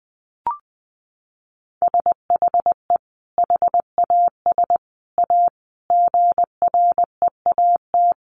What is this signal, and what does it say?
Morse code, heard as a keyed tone, about 20 wpm, 700 Hz. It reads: SHE HAS A GREAT
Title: Sentences from Top 1100 Words 20wpm